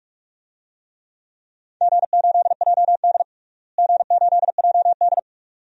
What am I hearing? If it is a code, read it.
Morse code, heard as a keyed tone, about 46 wpm, 700 Hz. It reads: G8JD G8JD
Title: DXCC-100 Call Signs - 4 Characters - Repeated 2x - 45wpm